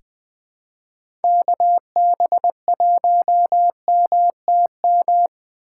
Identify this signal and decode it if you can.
Morse code, heard as a keyed tone, about 20 wpm, 700 Hz. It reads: KB1MTM
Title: USA Call Signs - 6x Character Spacing - 20wpm